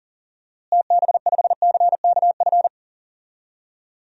Morse code, heard as a keyed tone, about 40 wpm, 700 Hz. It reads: TB5CKF
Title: DXCC-100 Call Signs - 6 Characters - Repeated 2x - 40wpm